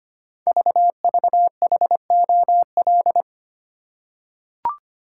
Morse code, heard as a keyed tone, about 25 wpm, 700 Hz. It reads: VVHOL
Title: Code Groups - 5 Letters - 25wpm